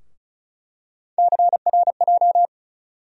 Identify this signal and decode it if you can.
Morse code, heard as a keyed tone, about 34 wpm, 700 Hz. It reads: CRJ
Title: USA Call Sign Postfixes - 3 Letters 35wpm